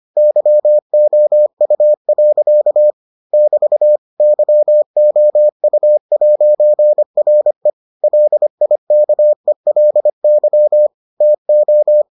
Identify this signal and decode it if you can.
Morse code, heard as a keyed tone, about 25 wpm, 600 Hz. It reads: YOU. <BT> YOU'RE LIKELY TO